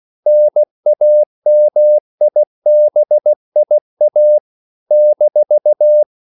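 Morse code, heard as a keyed tone, about 16 wpm, 600 Hz. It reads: NAMIBIA -
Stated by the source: CW practice of national capitals at 16 words per minute, from ON6ZQ.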